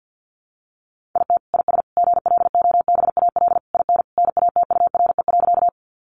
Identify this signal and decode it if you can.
Morse code, heard as a keyed tone, about 50 wpm, 700 Hz. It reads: IT IS GLOBAL IN NATURE.
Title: Sentences from Top 1200 Words 50wpm